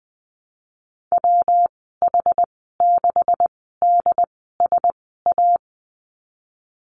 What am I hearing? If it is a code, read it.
Morse code, heard as a keyed tone, about 20 wpm, spaced as for 15 wpm, 700 Hz. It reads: WH6DSA
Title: 147 Lesson 24g - Letter y - Call Signs 20wpm